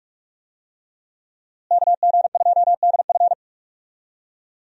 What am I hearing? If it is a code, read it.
Morse code, heard as a keyed tone, about 46 wpm, 700 Hz. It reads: KG2DF
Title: USA Call Signs - 5 Characters - Repeated 3x - 45wpm